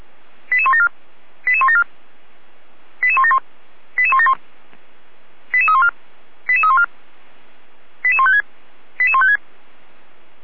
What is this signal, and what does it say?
Five-tone selective call (ZVEI), heard as quick runs of five tones, tones of about 70 ms. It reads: 88174 88174 88171 88171 88314 88314 88156 88156
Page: SDR/ZVEI/88122